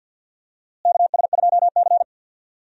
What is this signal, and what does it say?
Morse code, heard as a keyed tone, about 52 wpm, 700 Hz. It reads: KS2C
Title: USA Call Signs - 4 Characters - Repeated 2x - 50wpm